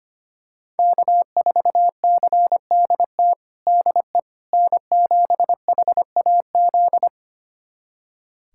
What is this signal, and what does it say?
Morse code, heard as a keyed tone, about 25 wpm, 700 Hz. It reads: K4CDT DE N7HAZ
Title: USA Callsign DE Callsign - 5 Characters - 25wpm